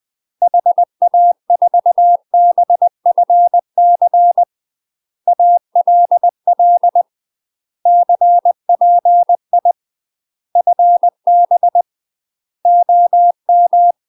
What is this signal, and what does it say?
Morse code, heard as a keyed tone, about 20 wpm, 700 Hz. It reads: HA4BFC ALL CPI FB OM
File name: UK MASTER QSOs TRAINING / 20WPM / UK MASTER QSOs 0055